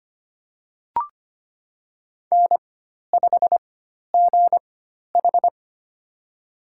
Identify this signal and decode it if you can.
Morse code, heard as a keyed tone, about 25 wpm, spaced as for 12 wpm, 700 Hz. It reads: N5GH
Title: USA Call Signs - 4 Characters - 4x Character Spacing - 25wpm